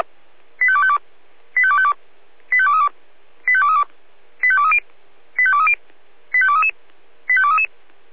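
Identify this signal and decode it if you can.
Five-tone selective call (ZVEI), heard as quick runs of five tones, tones of about 70 ms. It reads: 85252 85252 85232 85232 85230 85230 85233 85233
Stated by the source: ZVEI five-tone selective call received on SDR, code 85885.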